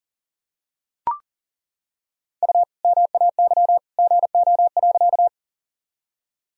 Morse code, heard as a keyed tone, about 40 wpm, 700 Hz. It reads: U MAY GO.
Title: Sentences from Top 100 Words and Abbreviations 40wpm